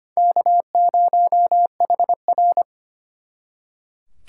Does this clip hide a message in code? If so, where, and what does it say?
Yes: Morse code from 0.2 to 2.6 s, heard as a keyed tone, about 32 wpm, 700 Hz. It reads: K0HR